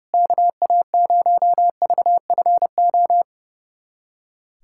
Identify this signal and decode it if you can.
Morse code, heard as a keyed tone, about 32 wpm, 700 Hz. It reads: KA0VFO
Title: USA Callsign DE Callsign - 6 Characters - 30wpm